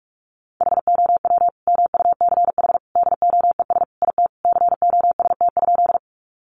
Morse code, heard as a keyed tone, about 44 wpm, 700 Hz. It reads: HOW MUCH DOES IT COST?
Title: Sentences from Top 800 Words 45wpm